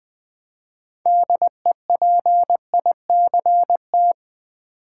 Morse code, heard as a keyed tone, about 20 wpm, 700 Hz. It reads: DEPICT
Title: 6 Letter Words - First 3 Letters Given - 20wpm